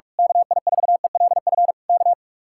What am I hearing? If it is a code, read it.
Morse code, heard as a keyed tone, about 46 wpm, 700 Hz. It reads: KI4ELF K
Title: Speed-Racing Common QSO Phrases - 45wpm